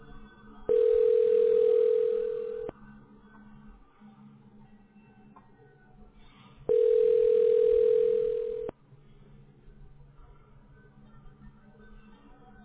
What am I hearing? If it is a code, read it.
Ringback tone (North America), 440 + 480 Hz.